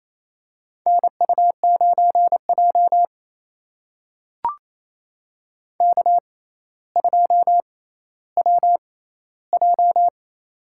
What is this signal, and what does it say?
Morse code, heard as a keyed tone, about 28 wpm, spaced as for 14 wpm, 700 Hz. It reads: NU9J K2WJ
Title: USA Call Signs - 4 Characters - 6x Character Spacing - 28wpm